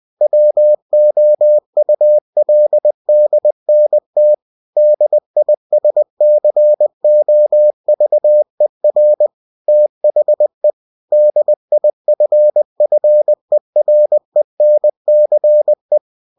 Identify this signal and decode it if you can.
Morse code, heard as a keyed tone, about 20 wpm, 600 Hz. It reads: WOULDNT DISCOVER THE DIFFERENCE
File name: cw-books / winnie-the-pooh / winnie-20wpm-page-0064